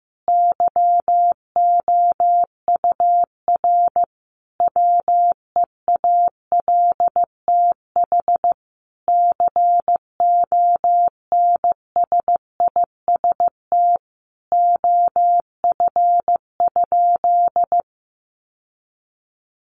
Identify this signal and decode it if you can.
Morse code, heard as a keyed tone, about 15 wpm, 700 Hz. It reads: YOUR WEALTH CONSIST OF?
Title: Sentences from Top 1400 Words 15wpm